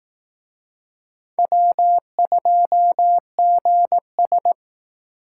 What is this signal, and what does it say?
Morse code, heard as a keyed tone, about 18 wpm, 700 Hz. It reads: W2GS
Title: Speed Racing - USA Call Signs 15wpm